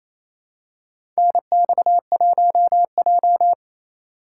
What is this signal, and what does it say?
Morse code, heard as a keyed tone, about 27 wpm, 700 Hz. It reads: NX1J
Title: USA Call Signs - 3x Character Spacing - 28wpm